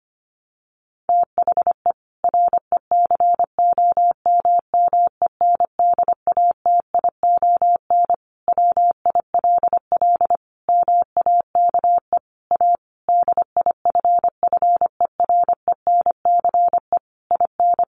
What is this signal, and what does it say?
Morse code, heard as a keyed tone, about 25 wpm, 700 Hz. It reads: THE RECOMMENDATION WILL MAKE A DIFFERENCE IN
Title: Sentences from Top 1400 Words 25wpm